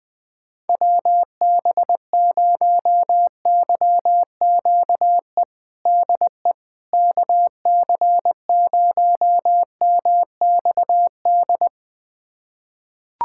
Morse code, heard as a keyed tone, about 20 wpm, 700 Hz. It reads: WB0YQE DE KC0MXD
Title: USA Callsign DE Callsign - 6 Characters - 20wpm